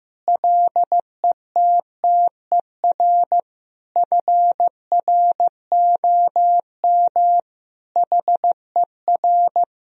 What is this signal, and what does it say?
Morse code, heard as a keyed tone, about 15 wpm, 700 Hz. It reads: LETTER FROM HER
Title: Sentences from Top 1000 Words - Review Part 3 - 15wpm